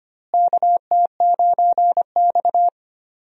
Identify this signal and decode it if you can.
Morse code, heard as a keyed tone, about 25 wpm, 700 Hz. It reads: KT9X
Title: USA Call Signs - 4 Characters - Repeated 2x - 25wpm